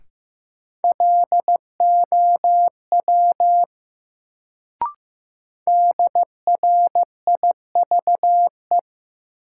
Morse code, heard as a keyed tone, about 15 wpm, 700 Hz. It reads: LOW DRIVE
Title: Sentences from Top 500 Words - New Word Drill - 15wpm